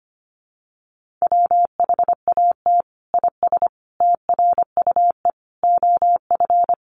Morse code, heard as a keyed tone, about 25 wpm, 700 Hz. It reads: WHAT IS TRUE OF
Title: Sentences from Top 1400 Words 25wpm